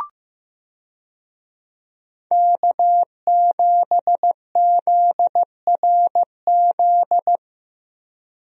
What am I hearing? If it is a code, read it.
Morse code, heard as a keyed tone, about 15 wpm, 700 Hz. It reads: K7ZRZ
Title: USA Call Signs - 5 Characters - Repeated 2x - 15wpm